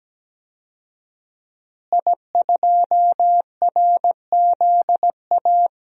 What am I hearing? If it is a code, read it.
Morse code, heard as a keyed tone, about 17 wpm, 700 Hz. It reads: I2RZA
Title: DXCC-100 Call Signs - 5 Characters - Repeated 2x - 17wpm